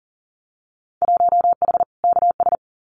Morse code, heard as a keyed tone, about 40 wpm, 700 Hz. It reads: 1H KS
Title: Winter Field Day Exchanges 40wpm